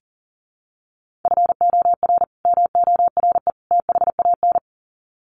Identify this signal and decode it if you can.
Morse code, heard as a keyed tone, about 40 wpm, 700 Hz. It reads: FOR MORE THAN
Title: Sets of 3 Words 40wpm